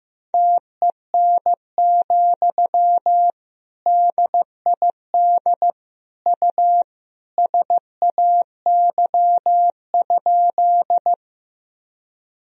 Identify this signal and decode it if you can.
Morse code, heard as a keyed tone, about 15 wpm, 700 Hz. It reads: TEN, DID U SAY?
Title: Sentences from Top 400 Words and Abbreviations 15wpm